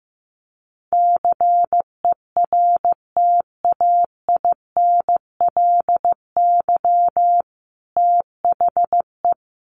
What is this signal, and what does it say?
Morse code, heard as a keyed tone, about 15 wpm, 700 Hz. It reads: CERTAINLY THE
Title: Sentences from Top 1200 Words 15wpm